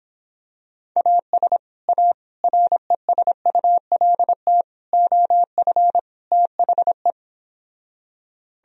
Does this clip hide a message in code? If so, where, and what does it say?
Yes: Morse code from 1.0 to 7.1 s, heard as a keyed tone, about 26 wpm, 700 Hz. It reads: AS A RESULT OF THE